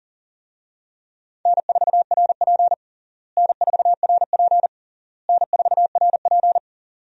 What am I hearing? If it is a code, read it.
Morse code, heard as a keyed tone, about 37 wpm, 700 Hz. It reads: N4RP N4RP N4RP
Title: USA Call Signs - 4 Characters - Repeated 3x - 40wpm